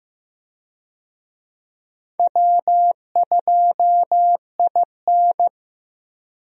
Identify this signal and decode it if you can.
Morse code, heard as a keyed tone, about 15 wpm, 700 Hz. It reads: W2IN
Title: USA Call Signs - 4 Characters - Repeated 3x - 15wpm